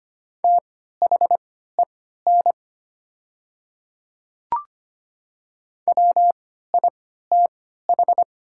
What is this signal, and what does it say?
Morse code, heard as a keyed tone, about 25 wpm, spaced as for 14 wpm, 700 Hz. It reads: THEN WITH
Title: 3x Character Spacing - Top 100 Words - 25wpm